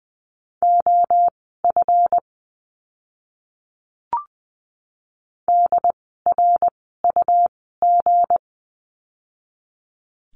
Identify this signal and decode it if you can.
Morse code, heard as a keyed tone, about 20 wpm, spaced as for 15 wpm, 700 Hz. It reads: OF DRUG
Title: 160 Lesson 26f - Letter g - Words 20wpm